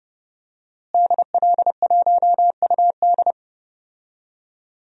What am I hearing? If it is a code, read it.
Morse code, heard as a keyed tone, about 30 wpm, 700 Hz. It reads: DL1UD
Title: DXCC-100 Call Signs - 5 Characters - Repeated 3x - 30wpm